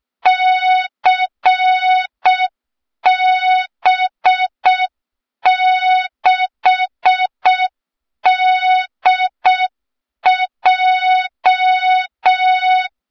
Morse code, heard as a keyed tone, about 6 wpm, 3700 Hz, with noise. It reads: CB6DJ
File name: Morze-HA2MN / hang / 23 GYAKORLAT